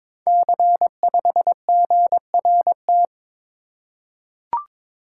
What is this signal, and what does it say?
Morse code, heard as a keyed tone, about 23 wpm, 690 Hz. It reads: C5GRT
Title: Code Groups - 5 Letter:Numbers - 22wpm